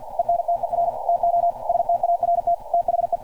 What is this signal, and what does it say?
Morse code, heard as a keyed tone, about 48 wpm, 690 Hz. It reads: ENTER TO START THE
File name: archived / CRSnet